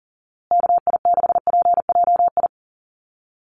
Morse code, heard as a keyed tone, about 40 wpm, 700 Hz. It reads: KI6PJI
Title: USA Call Signs Repeated 3x 40wpm